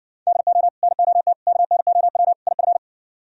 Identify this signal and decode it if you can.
Morse code, heard as a keyed tone, about 60 wpm, 700 Hz. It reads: DO NOT KNOW IF